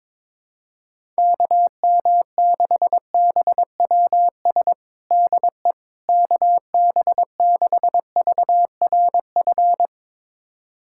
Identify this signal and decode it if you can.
Morse code, heard as a keyed tone, about 22 wpm, 700 Hz. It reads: KM6BWS DE KB6VRF
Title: USA Callsign DE Callsign - 6 Characters - 22wpm